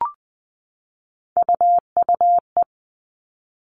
Morse code, heard as a keyed tone, about 20 wpm, 700 Hz. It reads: UUE
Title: All Cut Numbers - 3-Digit 20wpm